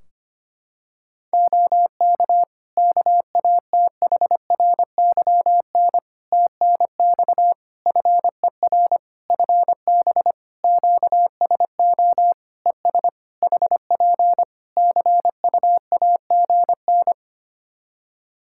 Morse code, heard as a keyed tone, about 25 wpm, 700 Hz. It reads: OK KATHRYN TNX FER FB QSO ES HP CUAGN
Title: Practice QSO - Exchange 4 - Ending - 25wpm